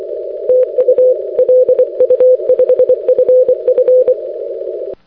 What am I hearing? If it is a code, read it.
Morse code, heard as a keyed tone, about 24 wpm, 510 Hz. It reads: TU LU5FF
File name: 6m-lu5ff